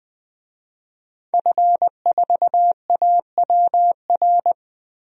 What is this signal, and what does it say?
Morse code, heard as a keyed tone, about 20 wpm, 700 Hz. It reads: F4AWR
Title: DXCC-100 Call Signs - 5 Characters - Repeated 2x - 20wpm